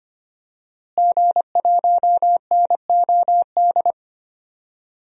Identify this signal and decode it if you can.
Morse code, heard as a keyed tone, about 25 wpm, 700 Hz. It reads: G1NOD
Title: DXCC-100 Call Signs - 5 Characters - Repeated 2x - 25wpm